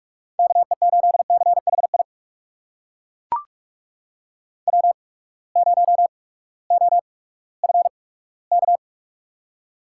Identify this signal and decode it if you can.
Morse code, heard as a keyed tone, about 42 wpm, spaced as for 20 wpm, 700 Hz. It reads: KE8CHI W0OFK
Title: USA Call Signs - 8x Character Spacing - 45wpm